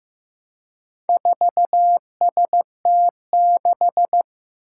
Morse code, heard as a keyed tone, about 15 wpm, 700 Hz. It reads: 4ST6
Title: Four Challenging Letter-Number Combinations 15wpm